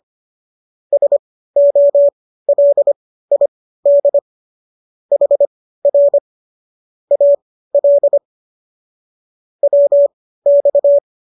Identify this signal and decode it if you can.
Morse code, heard as a keyed tone, about 25 wpm, spaced as for 15 wpm, 570 Hz. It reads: SOLID HR AL WX